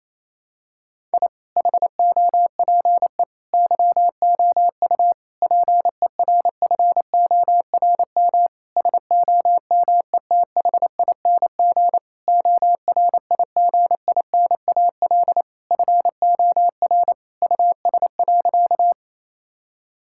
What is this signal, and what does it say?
Morse code, heard as a keyed tone, about 28 wpm, 700 Hz. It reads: I HOPE YOU PERFORM SOMETHING ORIGINAL FOR US.
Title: Sentences from Top 1000 Words - Review Part 6 - 28wpm